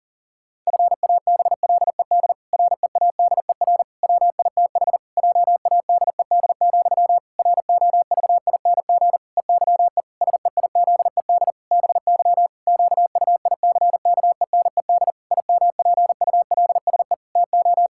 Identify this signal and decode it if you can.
Morse code, heard as a keyed tone, about 40 wpm, 700 Hz. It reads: FABLED READER WITH JADED, ROVING EYE SEIZED BY QUICKENED IMPULSE TO
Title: Pangram Challenge v2 - 40wpm